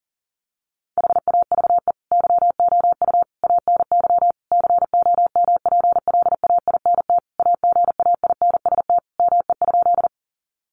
Morse code, heard as a keyed tone, about 40 wpm, 700 Hz. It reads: HAVE YOU ANY COMPLAINT AGAINST ME?
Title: Sentences from Top 1400 Words 40wpm